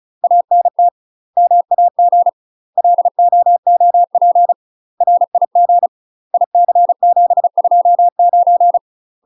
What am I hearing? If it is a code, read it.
Morse code, heard as a keyed tone, about 34 wpm, 700 Hz. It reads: ANT MAG LOOP RIG IC729
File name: UK MASTER QSOs TRAINING / 35WPM / UK MASTER QSOs 0004